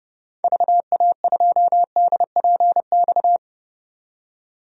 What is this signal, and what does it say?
Morse code, heard as a keyed tone, about 30 wpm, 700 Hz. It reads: VA2DPX
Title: DXCC-100 Call Signs - 6 Characters - 30wpm